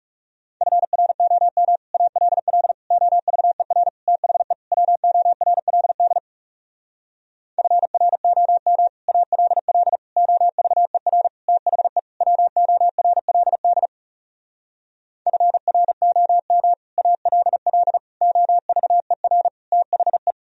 Morse code, heard as a keyed tone, about 37 wpm, 700 Hz. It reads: FROM ALL OVER THE WORLD FROM ALL OVER THE WORLD FROM ALL OVER THE
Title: Speed-Racing - Sets of 5 Words 30wpm